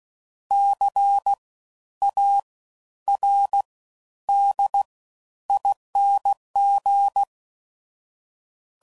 Morse code, heard as a keyed tone, about 16 wpm, spaced as for 10 wpm, 800 Hz. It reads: CARDING
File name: Suffixes / Special INGs / Fast Ings